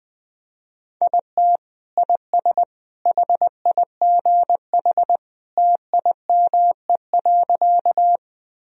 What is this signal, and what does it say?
Morse code, heard as a keyed tone, about 20 wpm, 700 Hz. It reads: IT IS HIGH TIME.
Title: Sentences from Top 1000 Words - Review Part 1 - 20wpm